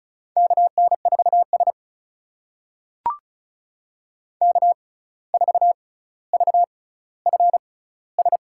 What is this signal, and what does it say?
Morse code, heard as a keyed tone, about 36 wpm, spaced as for 17 wpm, 700 Hz. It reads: KN4S K4VFS
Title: USA Call Signs - 6x Character Spacing - 35wpm